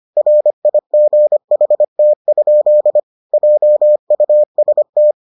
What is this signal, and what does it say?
Morse code, heard as a keyed tone, about 25 wpm, 600 Hz. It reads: RIGHT? JUST